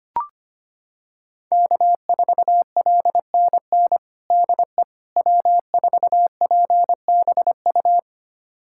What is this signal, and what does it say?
Morse code, heard as a keyed tone, about 25 wpm, 700 Hz. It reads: K4LNN DE W4PBU
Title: USA Callsign DE Callsign - 5 Characters - 25wpm